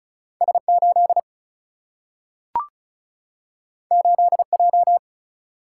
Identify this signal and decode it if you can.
Morse code, heard as a keyed tone, about 35 wpm, 700 Hz. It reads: S8 8J
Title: Single-Letter Single-Number 35wpm